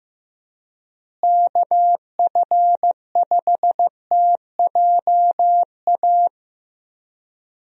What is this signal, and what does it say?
Morse code, heard as a keyed tone, about 15 wpm, 700 Hz. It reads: KF5TJA
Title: USA Call Signs - 6 Characters - Repeated 2x - 15wpm